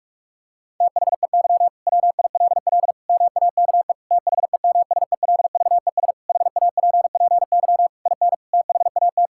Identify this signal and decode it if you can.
Morse code, heard as a keyed tone, about 46 wpm, 700 Hz. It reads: THEY WILL MAKE THEMSELVES HAPPY IN THAT
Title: Sentences from Top 1600 Words 45wpm